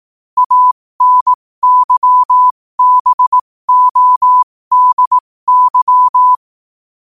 Morse code, heard as a keyed tone, about 18 wpm, spaced as for 15 wpm, 1000 Hz. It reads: ANYBODY
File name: qotdfiles / 15wpm